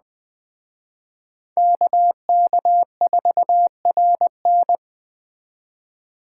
Morse code, heard as a keyed tone, about 20 wpm, 700 Hz. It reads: KK4RN